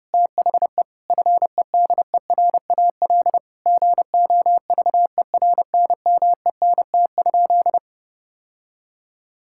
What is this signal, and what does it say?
Morse code, heard as a keyed tone, about 31 wpm, 700 Hz. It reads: THE FEDERAL GOVERNMENT?
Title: Sentences from Top 1000 Words - Review Part 5 - 30wpm